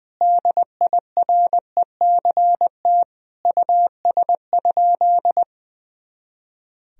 Morse code, heard as a keyed tone, about 20 wpm, 700 Hz. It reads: DIRECT US?
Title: Sentences from Top 400 Words and Abbreviations 20wpm